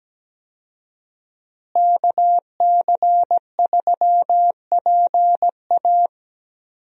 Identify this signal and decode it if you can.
Morse code, heard as a keyed tone, about 17 wpm, 700 Hz. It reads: KC3PA
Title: USA Call Signs - 5 Characters - Repeated 3x - 17wpm